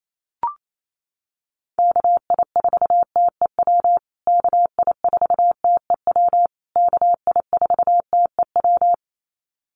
Morse code, heard as a keyed tone, about 28 wpm, 700 Hz. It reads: KI4TEW KI4TEW KI4TEW
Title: USA Call Signs Repeated 3x 28wpm